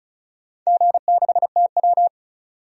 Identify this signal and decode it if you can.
Morse code, heard as a keyed tone, about 36 wpm, 700 Hz. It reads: G6TW
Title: DXCC-100 Call Signs - 4 Characters - 35wpm